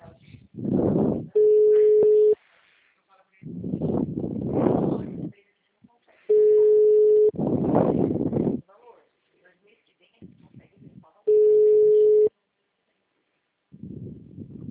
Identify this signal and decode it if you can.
Ringback tone (Europe, 425 Hz), with speech.
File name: out- -2015- - - .254